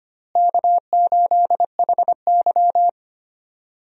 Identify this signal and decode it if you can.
Morse code, heard as a keyed tone, about 25 wpm, 700 Hz. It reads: K8HY
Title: USA Call Signs - 4 Characters - Repeated 2x - 25wpm